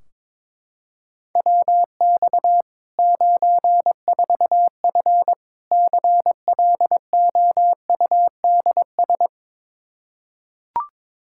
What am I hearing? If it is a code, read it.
Morse code, heard as a keyed tone, about 22 wpm, 700 Hz. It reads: WX 94F CLOUDS
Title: Practice QSO - Exchange 2 - Info - 22wpm